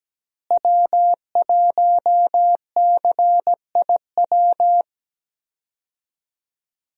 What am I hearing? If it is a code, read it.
Morse code, heard as a keyed tone, about 17 wpm, 700 Hz. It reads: W1CIW
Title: USA Call Signs - 5 Characters - Repeated 2x - 17wpm